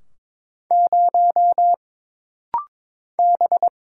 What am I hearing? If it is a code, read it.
Morse code, heard as a keyed tone, about 23 wpm, 700 Hz. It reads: 0 B
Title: Single Letter-Number Alt - Rapid-Fire 22wpm